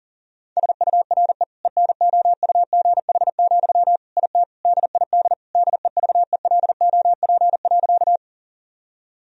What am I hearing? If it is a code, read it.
Morse code, heard as a keyed tone, about 40 wpm, 700 Hz. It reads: SURE ENOUGH, IT DID DEVELOP.
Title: Sentences from Top 1000 Words - Review Part 2 - 40wpm